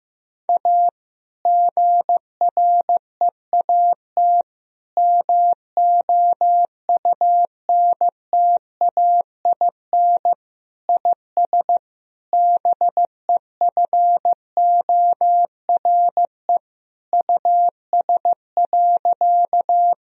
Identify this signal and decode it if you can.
Morse code, heard as a keyed tone, about 15 wpm, 700 Hz. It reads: A GREAT MOUNTAIN IS BEFORE US.
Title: Sentences from Top 1000 Words - Review Part 1 - 15wpm